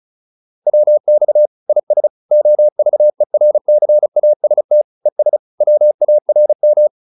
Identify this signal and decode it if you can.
Morse code, heard as a keyed tone, about 35 wpm, 600 Hz. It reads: WX IS OVERCAST ES WARM